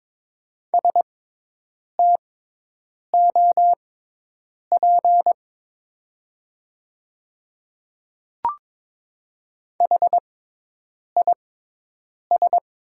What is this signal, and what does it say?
Morse code, heard as a keyed tone, about 22 wpm, spaced as for 8 wpm, 700 Hz. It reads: STOP HIS
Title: 6x Character Spacing - Top 300 Words - 22wpm